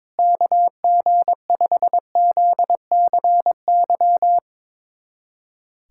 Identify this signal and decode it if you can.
Morse code, heard as a keyed tone, about 22 wpm, 700 Hz. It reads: KG5ZCY
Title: USA Callsign DE Callsign - 6 Characters - 22wpm